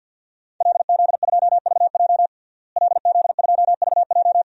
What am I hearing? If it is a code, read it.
Morse code, heard as a keyed tone, about 51 wpm, 700 Hz. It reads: LZ2VJ LZ2VJ
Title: DXCC-100 Call Signs - 5 Characters - Repeated 2x - 50wpm